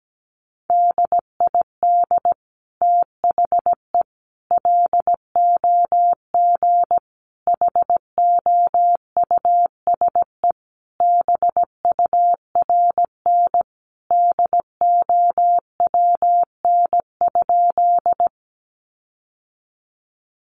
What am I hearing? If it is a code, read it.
Morse code, heard as a keyed tone, about 17 wpm, 700 Hz. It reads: DID THE LOG HOUSE BURN DOWN?
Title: Sentences from Top 1400 Words 17wpm